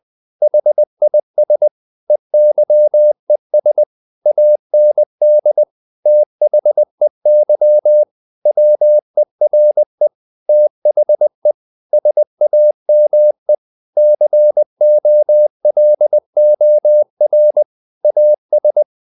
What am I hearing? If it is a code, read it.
Morse code, heard as a keyed tone, about 20 wpm, 600 Hz. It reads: HIS EYES AND THEY WERE THE SAME COLOR AS
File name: the old man and the sea / the old man and the sea part 1 20wpm